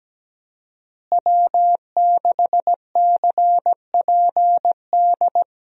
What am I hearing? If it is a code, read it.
Morse code, heard as a keyed tone, about 17 wpm, 700 Hz. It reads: W6CPD
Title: USA Call Signs - 5 Characters - Repeated 3x - 17wpm